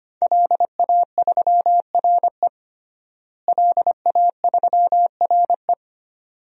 Morse code, heard as a keyed tone, about 25 wpm, 700 Hz. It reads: LA3RE LA3RE
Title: DXCC-100 Call Signs - 5 Characters - Repeated 2x - 25wpm